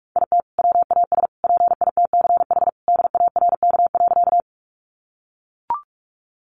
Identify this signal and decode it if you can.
Morse code, heard as a keyed tone, about 44 wpm, 700 Hz. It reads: IT WAS PITCH DARK.
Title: Sentences from Top 1300 Words 45wpm